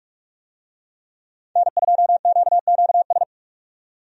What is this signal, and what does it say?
Morse code, heard as a keyed tone, about 42 wpm, 700 Hz. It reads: N2QQS
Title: USA Call Signs - 5 Characters - 45wpm